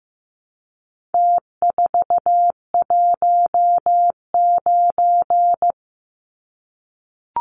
Morse code, heard as a keyed tone, about 15 wpm, 700 Hz. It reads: T419
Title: Numbers - Contest Style - 4-Digit 15wpm